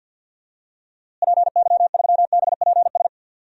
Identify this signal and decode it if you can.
Morse code, heard as a keyed tone, about 52 wpm, 700 Hz. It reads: PY3BPS
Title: DXCC-100 Call Signs - 6 Characters - Repeated 2x - 50wpm